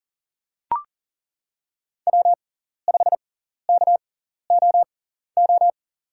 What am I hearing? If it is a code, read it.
Morse code, heard as a keyed tone, about 40 wpm, spaced as for 14 wpm, 700 Hz. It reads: W5KOO
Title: USA Call Signs - 5 Characters - 6x Character Spacing - 40wpm